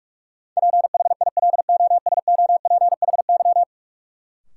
Morse code, heard as a keyed tone, about 42 wpm, 700 Hz. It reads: PHILOSOPHY
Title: Sentences from Top 900 Words - New Word Drill - 45wpm